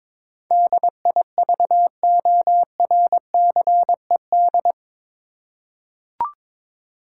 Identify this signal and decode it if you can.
Morse code, heard as a keyed tone, about 22 wpm, 700 Hz. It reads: DIVORCED
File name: practice / 22wpm / Eight Letter Words - 22wpm